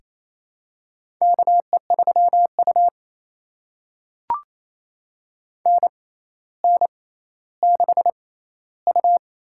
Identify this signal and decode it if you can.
Morse code, heard as a keyed tone, about 28 wpm, spaced as for 14 wpm, 700 Hz. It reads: KE3U NN6U
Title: USA Call Signs - 4 Characters - 6x Character Spacing - 28wpm